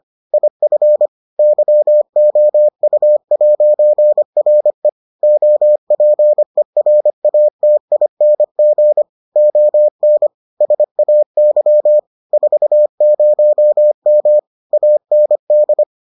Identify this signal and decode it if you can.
Morse code, heard as a keyed tone, about 25 wpm, 600 Hz. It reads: IF YOU'RE OPERATING ON SAY 40M AND